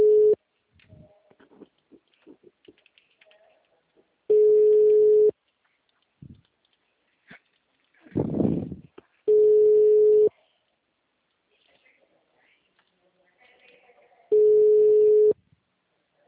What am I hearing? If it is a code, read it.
Ringback tone (Europe, 425 Hz).